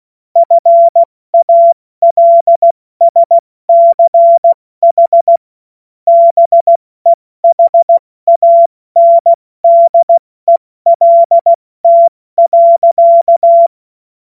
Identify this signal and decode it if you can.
Morse code, heard as a keyed tone, about 16 wpm, 680 Hz. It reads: FALSCH BEHANDELT.
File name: Morsekurs / Witze / Witz 01. Juni 2021, 70 F80